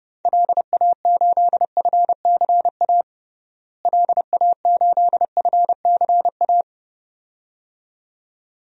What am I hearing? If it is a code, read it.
Morse code, heard as a keyed tone, about 29 wpm, 700 Hz. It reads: LA8FCA LA8FCA
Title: DXCC-100 Call Signs - 6 Characters - Repeated 2x - 30wpm